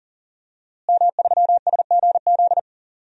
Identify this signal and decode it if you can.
Morse code, heard as a keyed tone, about 40 wpm, 700 Hz. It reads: M3SGZ
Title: DXCC-100 Call Signs - 5 Characters - Repeated 2x - 40wpm